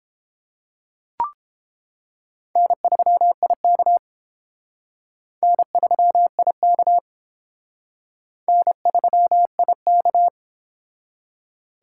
Morse code, heard as a keyed tone, about 30 wpm, 700 Hz. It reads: N3IK N3IK N3IK